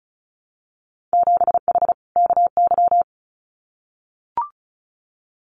Morse code, heard as a keyed tone, about 36 wpm, 700 Hz. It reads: 7H KY
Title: Winter Field Day Exchanges Encore 35wpm